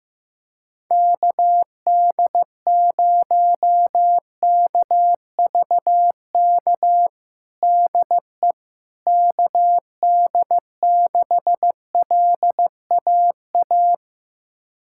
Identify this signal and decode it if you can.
Morse code, heard as a keyed tone, about 15 wpm, 700 Hz. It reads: KD0KVK DE KD6LAA
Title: USA Callsign DE Callsign - 6 Characters - 15wpm